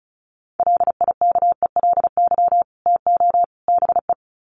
Morse code, heard as a keyed tone, about 34 wpm, 700 Hz. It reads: LIKELY TO BE
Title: Sets of 3 Words 35wpm